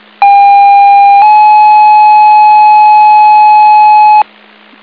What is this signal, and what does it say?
Two-tone page, heard as a one-second tone, then a three-second tone, about 800 then 850 Hz. It is heard from 0.2 to 4.2 s.
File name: Inlet Fire 09 36 13